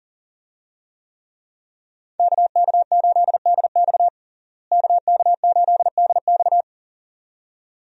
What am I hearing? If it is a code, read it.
Morse code, heard as a keyed tone, about 37 wpm, 700 Hz. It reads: KK8DX KK8DX
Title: USA Call Signs - 5 Characters - Repeated 2x - 40wpm